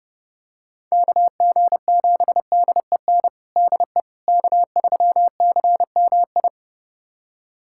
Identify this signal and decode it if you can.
Morse code, heard as a keyed tone, about 30 wpm, 700 Hz. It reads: KG7DEN DE K3CMI
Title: USA Callsign DE Callsign 30wpm